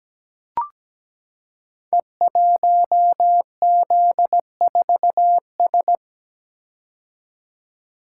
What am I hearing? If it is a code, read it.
Morse code, heard as a keyed tone, about 17 wpm, 700 Hz. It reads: E1Z4S
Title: Code Groups - 5 Letter:Numbers - 17wpm